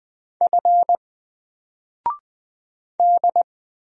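Morse code, heard as a keyed tone, about 20 wpm, 700 Hz. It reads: F D
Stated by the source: Single letter and number Morse drill at 20 wpm, from N4LCD.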